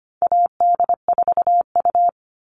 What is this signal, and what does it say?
Morse code, heard as a keyed tone, about 25 wpm, 700 Hz. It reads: AD4U
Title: All Cut Numbers - 4-Digit 25wpm